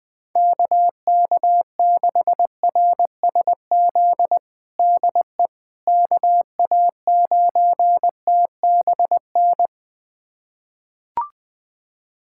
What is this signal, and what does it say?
Morse code, heard as a keyed tone, about 20 wpm, 700 Hz. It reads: KK6RSZ DE KA9TBN
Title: USA Callsign DE Callsign 20wpm